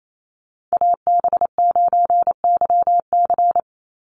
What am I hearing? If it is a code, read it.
Morse code, heard as a keyed tone, about 28 wpm, 700 Hz. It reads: AB9YC
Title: Call Signs Repeated 2x - 28wpm